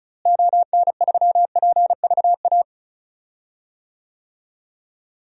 Morse code, heard as a keyed tone, about 34 wpm, 700 Hz. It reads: ON3PVA
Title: DXCC-100 Call Signs - 6 Characters - Repeated 2x - 35wpm